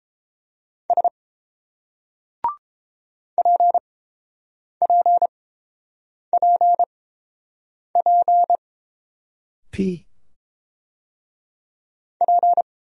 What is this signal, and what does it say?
Morse code, heard as a keyed tone, about 30 wpm, spaced as for 7 wpm, 700 Hz. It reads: S PPPP P